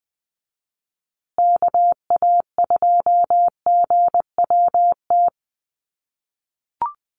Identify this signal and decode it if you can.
Morse code, heard as a keyed tone, about 20 wpm, 700 Hz. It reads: KA2GWT
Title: USA Call Signs Repeated 2x 20wpm